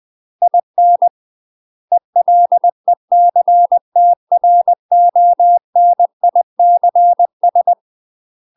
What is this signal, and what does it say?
Morse code, heard as a keyed tone, about 20 wpm, 700 Hz. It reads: IN ELECTRONICS